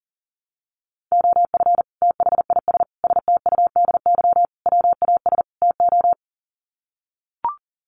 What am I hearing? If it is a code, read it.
Morse code, heard as a keyed tone, about 40 wpm, 700 Hz. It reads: OF THIS STUDY WAS TO
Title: Sets of 5 Words 40wpm